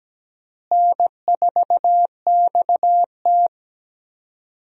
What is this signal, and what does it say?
Morse code, heard as a keyed tone, about 17 wpm, 700 Hz. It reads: N4XT
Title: USA Call Signs - 4 Characters - Repeated 2x - 17wpm